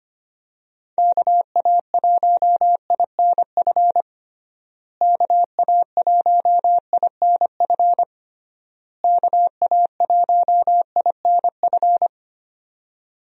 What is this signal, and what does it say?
Morse code, heard as a keyed tone, about 25 wpm, 700 Hz. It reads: KA1INF KA1INF KA1INF
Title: USA Call Signs - 6 Characters - Repeated 3x - 25wpm